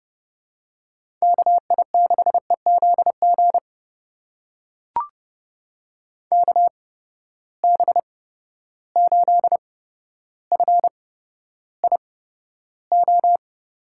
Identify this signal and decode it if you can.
Morse code, heard as a keyed tone, about 30 wpm, spaced as for 13 wpm, 700 Hz. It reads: KI6EZG KB8FIO
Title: USA Call Signs - 6 Characters - 8x Character Spacing - 30wpm